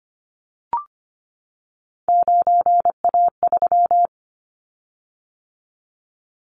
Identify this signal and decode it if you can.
Morse code, heard as a keyed tone, about 25 wpm, 700 Hz. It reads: 9A3
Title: Callsign Prefixes for Top 100 DXCC - 25wpm